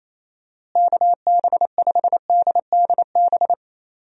Morse code, heard as a keyed tone, about 28 wpm, 700 Hz. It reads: KB5DDB
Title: USA Call Signs - 6 Characters - Repeated 2x - 28wpm